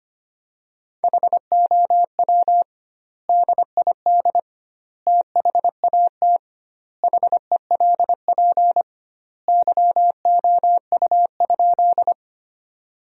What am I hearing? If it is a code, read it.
Morse code, heard as a keyed tone, about 25 wpm, 700 Hz. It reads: HOW DID THAT HELP YOU?
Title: 2x Word-Spacing - Sentences from Top 200 Words - 25wpm